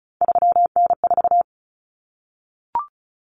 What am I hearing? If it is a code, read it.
Morse code, heard as a keyed tone, about 34 wpm, 700 Hz. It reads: 3N4
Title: Common Cut Numbers - 3-Digit 35wpm